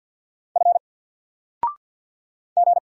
Morse code, heard as a keyed tone, about 50 wpm, 700 Hz. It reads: F G
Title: Single Letter-Number 50wpm